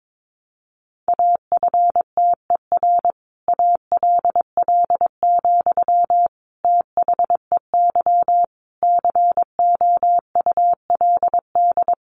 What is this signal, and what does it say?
Morse code, heard as a keyed tone, about 22 wpm, 700 Hz. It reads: AFTER ALL, THEY COULD
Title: Sentences from Top 1100 Words 22wpm